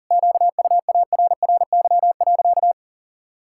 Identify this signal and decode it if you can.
Morse code, heard as a keyed tone, about 40 wpm, 700 Hz. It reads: QUARRY.
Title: Pangram Challenge v2 - 40wpm